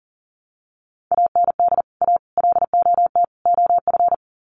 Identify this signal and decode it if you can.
Morse code, heard as a keyed tone, about 40 wpm, 700 Hz. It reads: AND A LOT OF
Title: Sets of 4 Words 40wpm